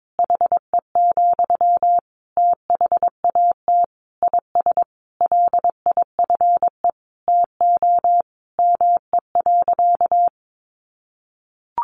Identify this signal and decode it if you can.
Morse code, heard as a keyed tone, about 22 wpm, 700 Hz. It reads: HE, THAT IS LIFE TO ME.
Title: Sentences from Top 500 Words 22wpm